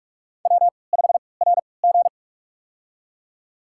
Morse code, heard as a keyed tone, about 46 wpm, spaced as for 25 wpm, 700 Hz. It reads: W5RG
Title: USA Call Signs - 3x Character Spacing - 45wpm